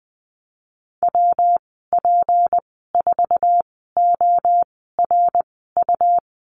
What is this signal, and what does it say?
Morse code, heard as a keyed tone, about 20 wpm, spaced as for 15 wpm, 700 Hz. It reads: WP4ORU
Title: 154 Lesson 25g - Letter p - Call Signs 20wpm